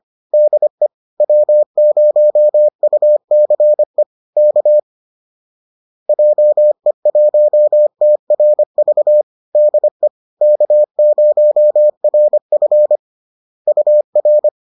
Morse code, heard as a keyed tone, about 25 wpm, 600 Hz. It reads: DE W0UCE K JE1TRV DE K0RF UR